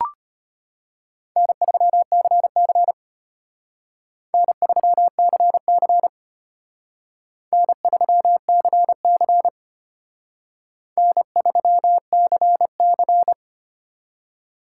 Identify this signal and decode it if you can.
Morse code, heard as a keyed tone, about 32 wpm, 700 Hz. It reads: N3CC N3CC N3CC N3CC